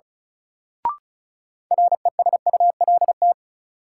Morse code, heard as a keyed tone, about 36 wpm, 700 Hz. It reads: RESULT
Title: Sets of 3 Words - New Word Drill - 35wpm